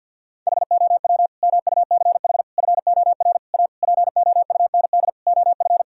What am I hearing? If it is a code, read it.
Morse code, heard as a keyed tone, about 52 wpm, 700 Hz. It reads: HOW MUCH FOR A POUND OF
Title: Sentences from Top 1000 Words - Review Part 5 - 50wpm